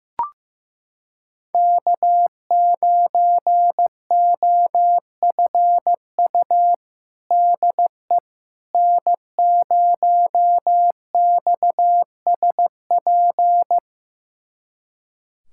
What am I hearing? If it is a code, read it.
Morse code, heard as a keyed tone, about 15 wpm, 700 Hz. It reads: K9OFU DE N0XSP
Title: USA Callsign DE Callsign - 5 Characters - 15wpm